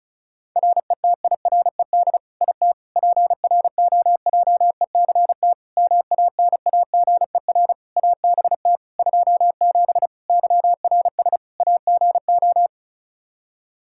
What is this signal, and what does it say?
Morse code, heard as a keyed tone, about 34 wpm, 700 Hz. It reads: RETIRED IT PROJECT MANAGER ABT 27 YRS AGO
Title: Practice QSO - Exchange 3 - More Info - 35wpm